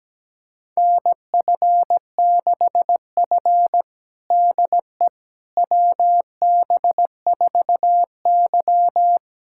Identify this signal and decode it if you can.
Morse code, heard as a keyed tone, about 17 wpm, 700 Hz. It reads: NF6F DE WB4Y
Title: USA Callsign DE Callsign - 4 Characters - 17wpm